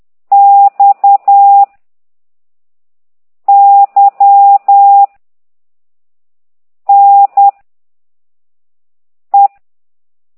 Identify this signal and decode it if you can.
Morse code, heard as a keyed tone, about 10 wpm, 800 Hz. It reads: X Y N E
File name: exerc7